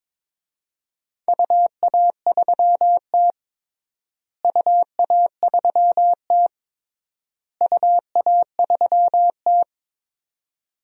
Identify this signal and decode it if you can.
Morse code, heard as a keyed tone, about 22 wpm, 700 Hz. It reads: UA3T UA3T UA3T
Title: DXCC-100 Call Signs - 4 Characters - Repeated 3x - 22wpm